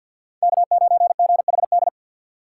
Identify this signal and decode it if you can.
Morse code, heard as a keyed tone, about 51 wpm, 700 Hz. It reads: K9GHD
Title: USA Call Signs - 5 Characters - 2x Character Spacing - 50wpm